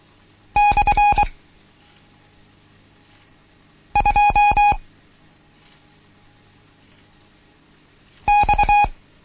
Morse code, heard as a keyed tone, about 23 wpm, 830 Hz. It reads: / 2 X
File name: Morsen / Kurs A 16